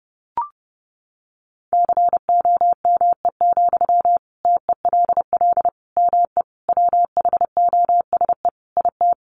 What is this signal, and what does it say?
Morse code, heard as a keyed tone, about 30 wpm, 700 Hz. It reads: COME, TELL ME WHOSE IT
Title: Sentences from Top 1200 Words 30wpm